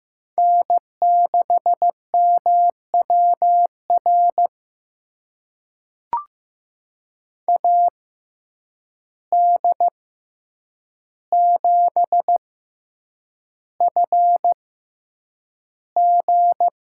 Morse code, heard as a keyed tone, about 15 wpm, spaced as for 8 wpm, 700 Hz. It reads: N6MWR AD7FG
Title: USA Call Signs - 5 Characters - 6x Character Spacing - 15wpm